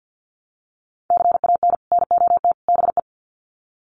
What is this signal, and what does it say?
Morse code, heard as a keyed tone, about 49 wpm, 700 Hz. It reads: CAN NOT BE
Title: Sets of 3 Words 50wpm